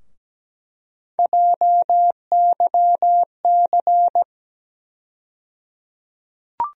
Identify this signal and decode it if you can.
Morse code, heard as a keyed tone, about 17 wpm, 700 Hz. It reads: JYC
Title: USA Call Sign Postfixes - 3 Letters 17wpm